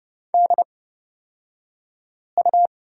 Morse code, heard as a keyed tone, about 30 wpm, spaced as for 5 wpm, 700 Hz. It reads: DU
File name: Lesson 37d - Letter x - Two Characters 30wpm